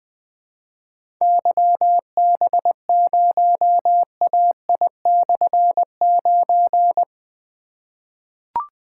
Morse code, heard as a keyed tone, about 20 wpm, 700 Hz. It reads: YB0AI/9
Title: Difficult Call Signs 20wpm